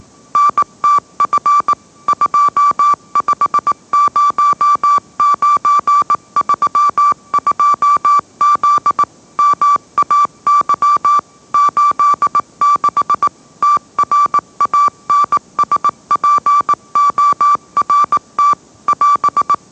Morse code, heard as a keyed tone, about 18 wpm, 1160 Hz. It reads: NTF 250932Z MAY 86 TRANSPORT <AS>